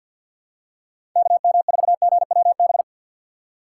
Morse code, heard as a keyed tone, about 53 wpm, 690 Hz. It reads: KM4GWB
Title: USA Call Signs - 6x Character Spacing - 50wpm